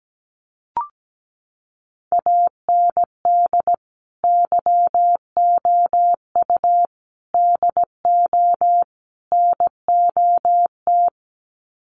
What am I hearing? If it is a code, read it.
Morse code, heard as a keyed tone, about 17 wpm, 700 Hz. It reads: AND YOU DO NOT
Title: Sets of 4 Words 17wpm